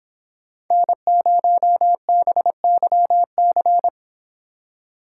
Morse code, heard as a keyed tone, about 24 wpm, 700 Hz. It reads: N0BYC